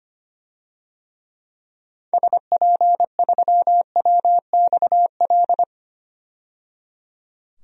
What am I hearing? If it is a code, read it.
Morse code, heard as a keyed tone, about 25 wpm, 690 Hz. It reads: SP3WXL
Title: DXCC-100 Call Signs - 6 Characters - 25wpm